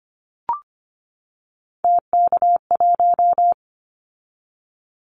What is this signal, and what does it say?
Morse code, heard as a keyed tone, about 25 wpm, 700 Hz. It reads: TK1
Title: Callsign Prefixes for Top 100 DXCC - 25wpm